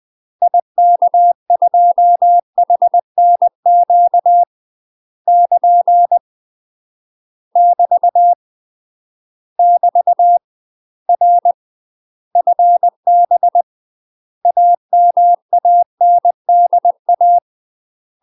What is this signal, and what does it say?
Morse code, heard as a keyed tone, about 20 wpm, 700 Hz. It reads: IK2HNQ <KN> <BT> <BT> R FB AMANDA
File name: UK MASTER QSOs TRAINING / 20WPM / UK MASTER QSOs 0023